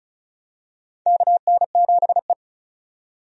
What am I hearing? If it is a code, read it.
Morse code, heard as a keyed tone, about 36 wpm, 700 Hz. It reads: KN7E
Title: USA Call Signs - 4 Characters - 35wpm